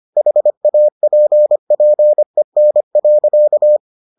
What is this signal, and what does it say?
Morse code, heard as a keyed tone, about 25 wpm, 600 Hz. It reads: HAPPEN.